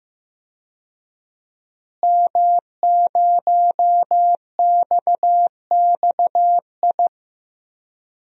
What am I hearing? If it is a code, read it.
Morse code, heard as a keyed tone, about 15 wpm, 700 Hz. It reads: M0XXI
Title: DXCC-100 Call Signs - 5 Characters - Repeated 3x - 15wpm